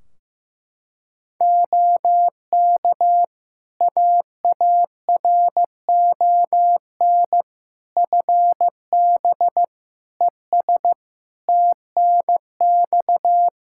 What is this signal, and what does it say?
Morse code, heard as a keyed tone, about 15 wpm, 700 Hz. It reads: OK AARON FB ES TNX
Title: K1USN SST QSO Practice 15wpm